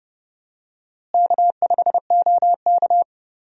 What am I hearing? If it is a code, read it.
Morse code, heard as a keyed tone, about 30 wpm, 700 Hz. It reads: K5OK
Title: USA Call Signs - 4 Characters - Repeated 2x - 30wpm